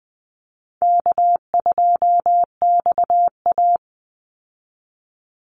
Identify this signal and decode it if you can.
Morse code, heard as a keyed tone, about 20 wpm, 700 Hz. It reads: K2XA
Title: Easy Call Signs Repeated 3x - 20wpm